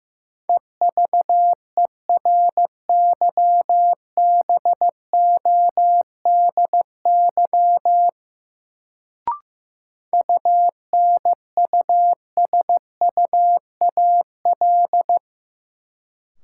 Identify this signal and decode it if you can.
Morse code, heard as a keyed tone, about 15 wpm, 700 Hz. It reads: EVERYBODY UNUSUAL
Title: Sentences from Top 1400 Words - New Word Drill - 15wpm